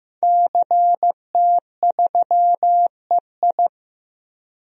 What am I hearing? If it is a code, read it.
Morse code, heard as a keyed tone, about 15 wpm, 700 Hz. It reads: CT3EI
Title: DXCC-100 Call Signs - 5 Characters - Repeated 2x - 15wpm